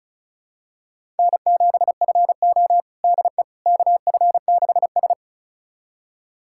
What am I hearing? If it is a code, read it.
Morse code, heard as a keyed tone, about 36 wpm, 700 Hz. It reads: N7FO DE KF6S
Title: USA Callsign DE Callsign - 4 Characters - 35wpm